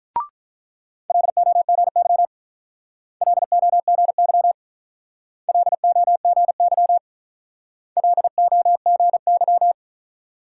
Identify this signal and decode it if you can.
Morse code, heard as a keyed tone, about 44 wpm, 700 Hz. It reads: LOGY LOGY LOGY LOGY